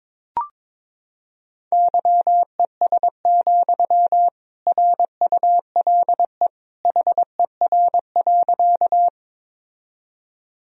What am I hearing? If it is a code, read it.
Morse code, heard as a keyed tone, about 22 wpm, 700 Hz. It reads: YES, RULE HER.
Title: Sentences from Top 1000 Words - Review Part 2 - 22wpm